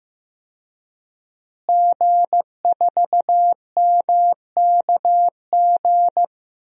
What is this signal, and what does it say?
Morse code, heard as a keyed tone, about 15 wpm, 700 Hz. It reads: G4MKG
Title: DXCC-100 Call Signs - 5 Characters - Repeated 3x - 15wpm